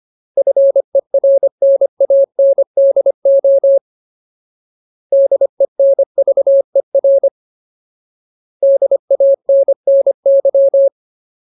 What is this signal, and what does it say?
Morse code, heard as a keyed tone, about 25 wpm, 550 Hz. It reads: FERNANDO DENVER DANNY
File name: Ultimate Word List / 25WPM Ultimate